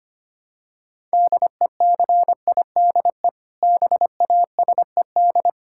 Morse code, heard as a keyed tone, about 25 wpm, 700 Hz. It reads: DECIDE BASED
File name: practice / 25wpm / Sentences with 2-Word Phrases - 25wpm